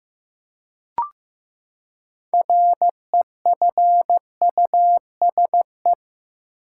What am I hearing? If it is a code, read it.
Morse code, heard as a keyed tone, about 15 wpm, 700 Hz. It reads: REFUSE
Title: Six Letter Words - 15wpm